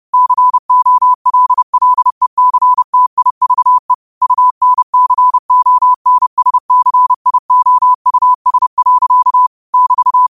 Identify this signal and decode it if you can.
Morse code, heard as a keyed tone, about 30 wpm, 1000 Hz. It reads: COLLECTIVE UNCONSCIOUS. <BT>
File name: qotdfiles / 30wpm